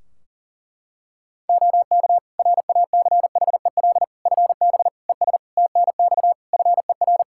Morse code, heard as a keyed tone, about 40 wpm, 700 Hz. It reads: OK RACHEL FB ES TNX FER
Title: Practice QSO - Exchange 2 - Info - 40wpm